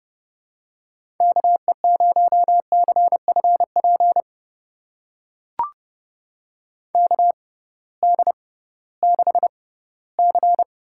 Morse code, heard as a keyed tone, about 31 wpm, spaced as for 18 wpm, 700 Hz. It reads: KE0CFP KD6C
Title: USA Call Signs - 6 Characters - 6x Character Spacing - 30wpm